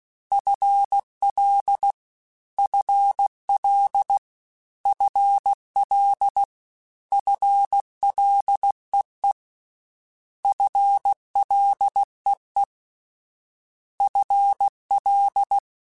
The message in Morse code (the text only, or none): FL FL FL FLEE FLEE FL